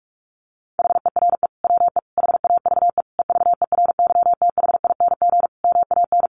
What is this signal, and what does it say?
Morse code, heard as a keyed tone, about 44 wpm, 700 Hz. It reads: HERE WE HAVE EVERYTHING MAN
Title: Sentences from Top 1100 Words 45wpm